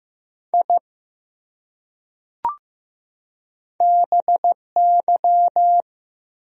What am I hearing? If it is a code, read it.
Morse code, heard as a keyed tone, about 15 wpm, 700 Hz. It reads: I BY
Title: Top 100 Words 15wpm